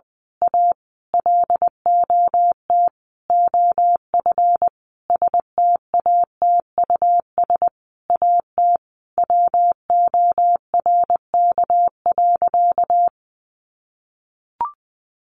Morse code, heard as a keyed tone, about 20 wpm, 700 Hz. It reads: A LOT OF STATUS AT WORK.
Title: Sentences from Top 1400 Words 20wpm